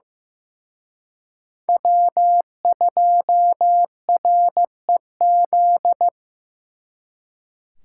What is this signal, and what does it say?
Morse code, heard as a keyed tone, about 18 wpm, 700 Hz. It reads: W2REZ
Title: Speed Racing - USA 1x3 Call Signs 15wpm